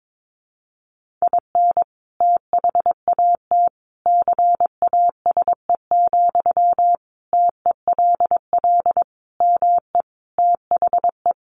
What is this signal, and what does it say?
Morse code, heard as a keyed tone, about 22 wpm, 700 Hz. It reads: IN THAT CASE, TELL ME THE
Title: Sentences from Top 1400 Words 22wpm